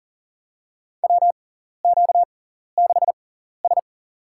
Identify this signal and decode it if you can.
Morse code, heard as a keyed tone, about 40 wpm, spaced as for 14 wpm, 700 Hz. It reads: WQ6S
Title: USA Call Signs - 4 Characters - 6x Character Spacing - 40wpm